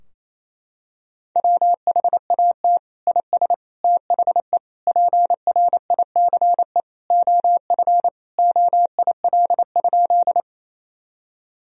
Morse code, heard as a keyed tone, about 28 wpm, 700 Hz. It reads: WHAT IS THE PRICE OF OIL?
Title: Sentences from Top 1000 Words - Review Part 4 - 28wpm